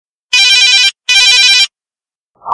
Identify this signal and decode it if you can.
There is no code.